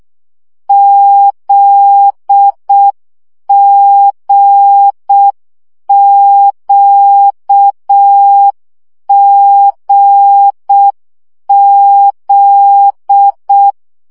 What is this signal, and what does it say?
Morse code, heard as a keyed tone, about 6 wpm, 800 Hz. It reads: ZGQGZ
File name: CW gyakorlas g z q 5 baud